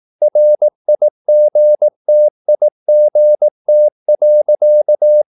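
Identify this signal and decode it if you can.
Morse code, heard as a keyed tone, about 18 wpm, 600 Hz. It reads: RIGTIGT.
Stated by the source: Slow (QRS) Morse practice at 18 wpm, CW med Gnister.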